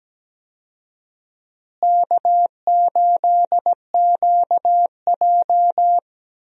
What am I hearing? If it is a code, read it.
Morse code, heard as a keyed tone, about 17 wpm, 700 Hz. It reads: K8QJ
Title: USA Call Signs - 4 Characters - Repeated 3x - 17wpm